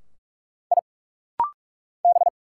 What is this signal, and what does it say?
Morse code, heard as a keyed tone, about 46 wpm, 690 Hz. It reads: I B
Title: Single Letters - Rapid-Fire 45wpm